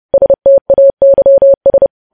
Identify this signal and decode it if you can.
Morse code, heard as a keyed tone, about 30 wpm, 550 Hz. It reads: STAYS
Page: News Headlines in Morse Code at 30 WPM